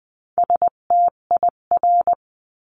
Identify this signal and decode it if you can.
Morse code, heard as a keyed tone, about 20 wpm, 700 Hz. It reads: STIR
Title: 258 Lesson 42c - Words 16-20wpm